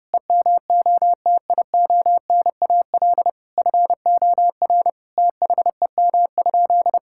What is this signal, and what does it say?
Morse code, heard as a keyed tone, about 32 wpm, 700 Hz. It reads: EMOTIONAL FOR THEM?
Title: Sentences from Top 1000 Words - Review Part 5 - 30wpm